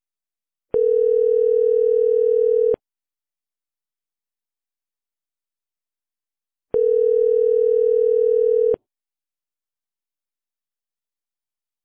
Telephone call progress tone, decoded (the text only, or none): ringback tone